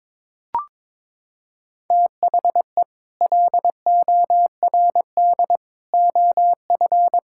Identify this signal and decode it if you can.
Morse code, heard as a keyed tone, about 22 wpm, 700 Hz. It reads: THE LORD OF
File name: Sets of 9 Words - 22wpm